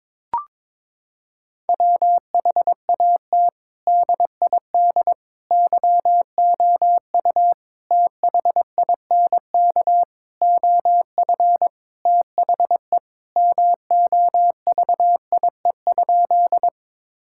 Morse code, heard as a keyed tone, about 22 wpm, 700 Hz. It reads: WHAT DID YOU THINK OF THE MOVIE?
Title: Sentences from Top 1000 Words - Review Part 5 - 22wpm